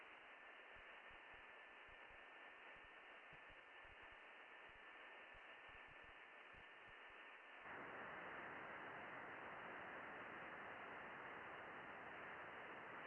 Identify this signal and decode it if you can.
FT8.
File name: contests / june-vhf-k5tr- -0200